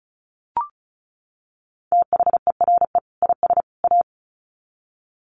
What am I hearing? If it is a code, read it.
Morse code, heard as a keyed tone, about 34 wpm, 700 Hz. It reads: THERE IS A
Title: Sets of 3 Words 35wpm